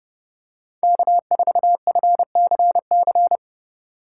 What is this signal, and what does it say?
Morse code, heard as a keyed tone, about 30 wpm, 700 Hz. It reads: K4FCC
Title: K1USN SST QSO Practice - 30wpm